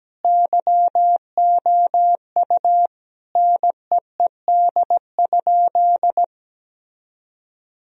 Morse code, heard as a keyed tone, about 17 wpm, 700 Hz. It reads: YOU NEED?
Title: Sentences from Top 1000 Words - Review Part 4 - 17wpm